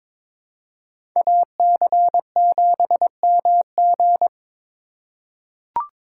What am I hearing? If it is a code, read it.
Morse code, heard as a keyed tone, about 22 wpm, 700 Hz. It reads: AC7MG
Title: USA Call Signs - 2x Character Spacing - 22wpm